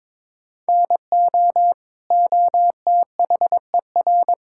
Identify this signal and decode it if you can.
Morse code, heard as a keyed tone, about 22 wpm, 700 Hz. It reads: NO OTHER
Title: Sentences from Top 1000 Words - Review Part 5 - 22wpm